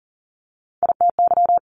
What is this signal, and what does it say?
Morse code, heard as a keyed tone, about 40 wpm, 700 Hz. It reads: ITY
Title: Multi-letter Phonemes 40wpm